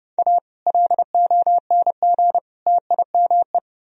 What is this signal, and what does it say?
Morse code, heard as a keyed tone, about 30 wpm, 700 Hz. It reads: A LONG TIME